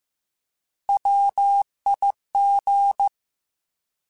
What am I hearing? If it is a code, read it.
Morse code, heard as a keyed tone, about 15 wpm, 800 Hz. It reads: WIG
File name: Alphabetical / Alphabetical mp3s / Ws @13